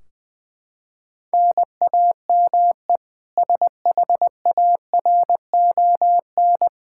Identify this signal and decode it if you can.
Morse code, heard as a keyed tone, about 21 wpm, 700 Hz. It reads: NAME SHARON
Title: Practice QSO - Exchange 1 - Beginning - 20wpm